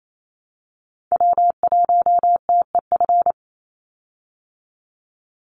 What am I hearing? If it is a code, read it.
Morse code, heard as a keyed tone, about 28 wpm, 700 Hz. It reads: W1TEF
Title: Call Signs Repeated 2x - 28wpm